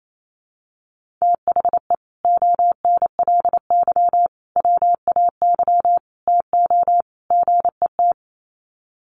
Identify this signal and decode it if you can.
Morse code, heard as a keyed tone, about 28 wpm, 700 Hz. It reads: THE ONLY WAY TO GET
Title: Sets of 5 Words Encore2 28wpm